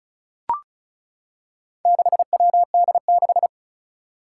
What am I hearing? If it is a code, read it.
Morse code, heard as a keyed tone, about 36 wpm, 700 Hz. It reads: 6WD6
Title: Four Challenging Letter-Number Combinations 35wpm